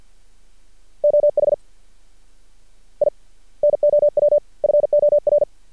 Morse code, heard as a keyed tone, about 48 wpm, 590 Hz. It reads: OH I NOW FOR
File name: archived / CRSnet